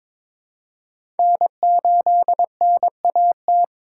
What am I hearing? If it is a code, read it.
Morse code, heard as a keyed tone, about 21 wpm, 700 Hz. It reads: N8NAT
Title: USA Call Signs - 4x Character Spacing - 22wpm